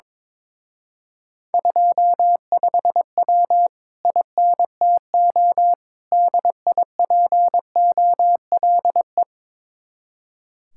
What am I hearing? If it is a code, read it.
Morse code, heard as a keyed tone, about 22 wpm, 700 Hz. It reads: 25W INTO DIPOLE
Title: Speed-Racing Common QSO Phrases Encore - 22wpm